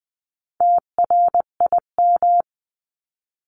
Morse code, heard as a keyed tone, about 20 wpm, 700 Hz. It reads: TRIM
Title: 259 Lesson 42d - Words 18-20wpm